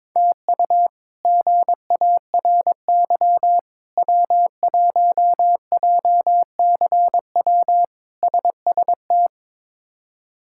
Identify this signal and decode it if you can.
Morse code, heard as a keyed tone, about 22 wpm, 700 Hz. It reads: TU GARY W1JCW SST
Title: K1USN SST QSO Practice - 22wpm